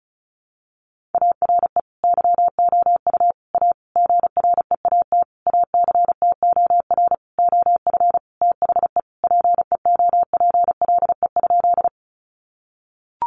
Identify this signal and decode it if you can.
Morse code, heard as a keyed tone, about 34 wpm, 700 Hz. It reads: ARE YOU A GREAT ACTOR OF THE PEOPLE?
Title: Sentences from Top 1500 Words 35wpm